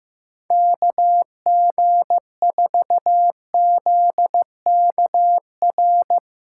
Morse code, heard as a keyed tone, about 15 wpm, 700 Hz. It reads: KG4ZKR
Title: USA Call Signs - 6 Characters - 8x Character Spacing - 15wpm